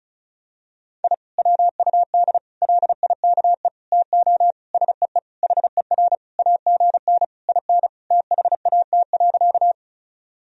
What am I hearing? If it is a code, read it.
Morse code, heard as a keyed tone, about 35 wpm, 700 Hz. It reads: I WUD LIKE TO SEE HER AGN IN THAT.
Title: Sentences from Top 200 Words and Abbreviations 35wpm